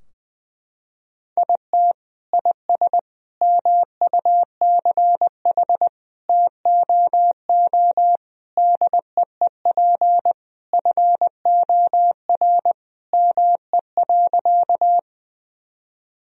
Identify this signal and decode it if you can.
Morse code, heard as a keyed tone, about 20 wpm, 700 Hz. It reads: IT IS MUCH TOO DEEP FOR ME.
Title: Sentences from Top 1000 Words - Review Part 3 - 20wpm